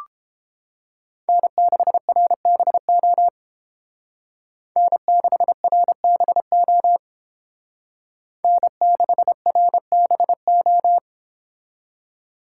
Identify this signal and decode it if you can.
Morse code, heard as a keyed tone, about 29 wpm, 700 Hz. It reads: N6RBO N6RBO N6RBO